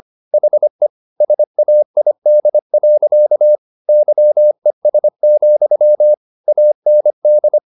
Morse code, heard as a keyed tone, about 25 wpm, 600 Hz. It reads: HE SAID. YES, AND
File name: the old man and the sea part 1 25wpm